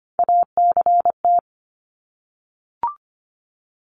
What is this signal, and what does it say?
Morse code, heard as a keyed tone, about 25 wpm, 700 Hz. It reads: ACT
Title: Top 300 Words 25wpm